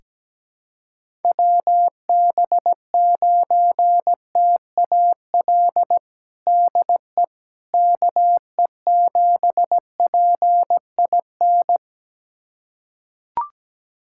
Morse code, heard as a keyed tone, about 17 wpm, 700 Hz. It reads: WB9TAL DE KE7PIN
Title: USA Callsign DE Callsign - 6 Characters - 17wpm